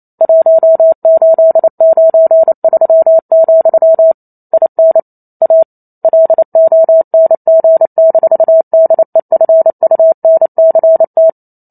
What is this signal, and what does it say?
Morse code, heard as a keyed tone, about 28 wpm, 650 Hz. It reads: 1893, IN A LONG-DEFUNCT